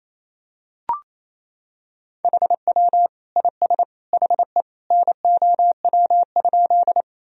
Morse code, heard as a keyed tone, about 28 wpm, 700 Hz. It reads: HW IS HE NOW?
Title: Sentences from Top 100 Words and Abbreviations 28wpm